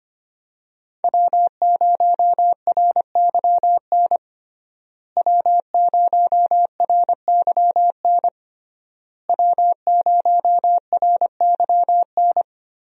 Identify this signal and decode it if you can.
Morse code, heard as a keyed tone, about 25 wpm, 700 Hz. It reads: W0RYN W0RYN W0RYN
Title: USA Call Signs - 5 Characters - Repeated 3x - 25wpm